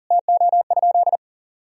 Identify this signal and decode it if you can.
Morse code, heard as a keyed tone, about 40 wpm, 700 Hz. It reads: TO?
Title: Sentences from Top 1000 Words - Review Part 4 - 40wpm